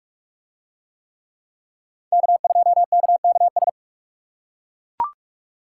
Morse code, heard as a keyed tone, about 40 wpm, 700 Hz. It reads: K2KKS